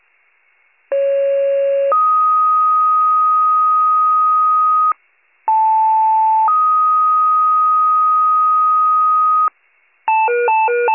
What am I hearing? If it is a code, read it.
Two-tone page, heard as a one-second tone, then a three-second tone, about 550 then 1250 Hz, about 880 then 1250 Hz. It is heard from 0.9 to 9.5 s.